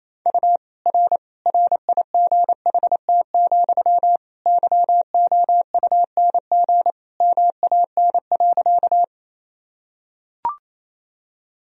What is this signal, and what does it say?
Morse code, heard as a keyed tone, about 28 wpm, 700 Hz. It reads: U R RIGHT, YOUNG MAN.
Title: Sentences from Top 400 Words and Abbreviations 28wpm